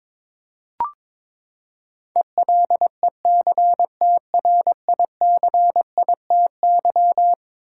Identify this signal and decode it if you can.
Morse code, heard as a keyed tone, about 22 wpm, 700 Hz. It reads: ELECTRICITY
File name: practice / 22wpm / Eleven Letter Words - 22wpm